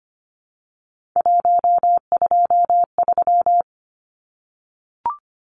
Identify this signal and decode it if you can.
Morse code, heard as a keyed tone, about 25 wpm, 700 Hz. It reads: 123
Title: Numbers - 3-Digit 25wpm